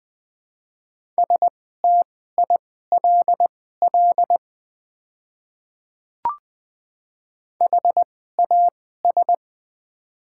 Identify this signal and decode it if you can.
Morse code, heard as a keyed tone, about 20 wpm, spaced as for 15 wpm, 690 Hz. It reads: STILL HAS
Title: 2x Character Spacing - Top 300 Words - 20wpm